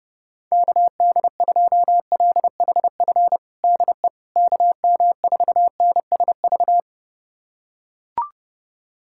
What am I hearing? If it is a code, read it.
Morse code, heard as a keyed tone, about 30 wpm, 700 Hz. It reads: KD2LHF DE KM4NSV
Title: USA Callsign DE Callsign - 6 Characters - 30wpm